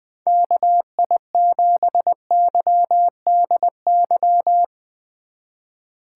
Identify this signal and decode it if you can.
Morse code, heard as a keyed tone, about 20 wpm, 700 Hz. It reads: KI7YDY
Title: USA Call Signs - 2x Character Spacing - 20wpm